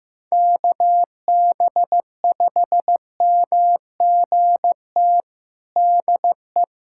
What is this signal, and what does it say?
Morse code, heard as a keyed tone, about 16 wpm, 700 Hz. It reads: KB5MGT DE
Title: USA Callsign DE Callsign - 6 Characters - 15wpm